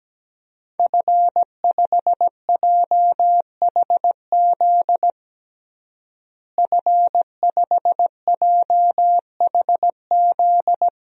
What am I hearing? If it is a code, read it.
Morse code, heard as a keyed tone, about 17 wpm, 700 Hz. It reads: F5JHZ F5JHZ
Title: DXCC-100 Call Signs - 5 Characters - Repeated 2x - 17wpm